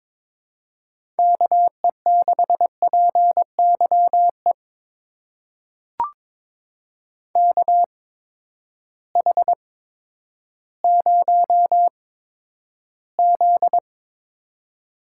Morse code, heard as a keyed tone, about 22 wpm, spaced as for 11 wpm, 700 Hz. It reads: KE6PYE KH0Z
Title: USA Call Signs - 8x Character Spacing - 22wpm